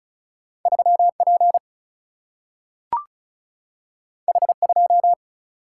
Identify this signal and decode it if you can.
Morse code, heard as a keyed tone, about 36 wpm, 700 Hz. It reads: 3P H2
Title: Single-Letter Single-Number 35wpm